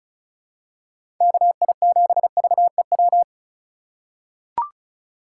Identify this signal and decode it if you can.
Morse code, heard as a keyed tone, about 34 wpm, 700 Hz. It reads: KI7VEW
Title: USA Call Signs - 6 Characters - 2x Character Spacing - 35wpm